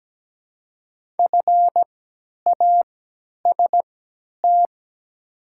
Morse code, heard as a keyed tone, about 17 wpm, spaced as for 10 wpm, 700 Hz. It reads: FAST
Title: 3x Character Spacing - Top 500 Words - 17wpm